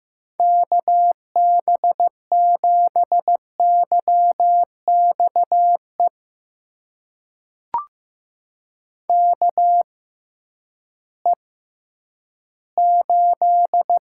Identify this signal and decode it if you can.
Morse code, heard as a keyed tone, about 15 wpm, spaced as for 10 wpm, 700 Hz. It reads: KB7YXE KE8
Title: USA Call Signs - 6 Characters - 6x Character Spacing - 15wpm